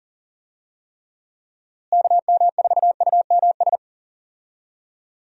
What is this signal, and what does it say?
Morse code, heard as a keyed tone, about 40 wpm, 700 Hz. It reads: KM4UMS
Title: USA Call Signs - 6 Characters - Repeated 2x - 40wpm